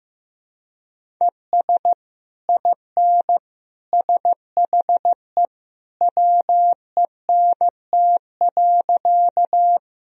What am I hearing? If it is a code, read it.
Morse code, heard as a keyed tone, about 15 wpm, 700 Hz. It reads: ES IN SHE WENT.
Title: Sentences from Top 200 Words and Abbreviations 15wpm